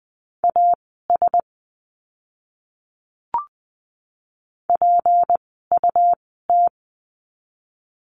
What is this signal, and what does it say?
Morse code, heard as a keyed tone, about 19 wpm, spaced as for 14 wpm, 700 Hz. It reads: AS PUT
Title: 244 Lesson 39f - Letter z - Words 20wpm